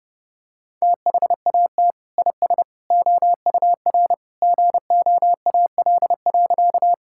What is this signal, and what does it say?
Morse code, heard as a keyed tone, about 30 wpm, 700 Hz. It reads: THAT IS OUR GOAL.
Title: Sentences from Top 1000 Words - Review Part 5 - 30wpm